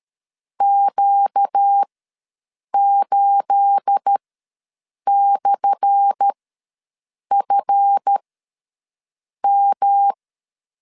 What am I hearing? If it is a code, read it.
Morse code, heard as a keyed tone, about 13 wpm, spaced as for 6 wpm, 800 Hz. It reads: Q8/FM
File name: LEZIONI / 10 SIGNS / 10A SLASH ws 12wpm 3443